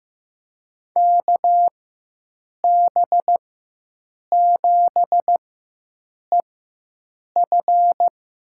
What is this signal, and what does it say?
Morse code, heard as a keyed tone, about 15 wpm, spaced as for 7 wpm, 700 Hz. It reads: KB7EF
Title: USA Call Signs - 6 Characters - 4x Character Spacing - 15wpm